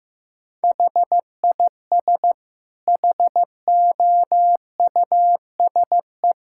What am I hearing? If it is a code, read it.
Morse code, heard as a keyed tone, about 15 wpm, 700 Hz. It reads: HIS HOUSE
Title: Sentences from Top 1000 Words - Review Part 6 - 15wpm